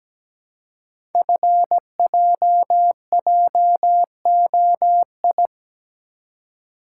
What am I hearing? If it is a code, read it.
Morse code, heard as a keyed tone, about 17 wpm, 700 Hz. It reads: FJJOI
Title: Code Groups - 5 Letters - 17wpm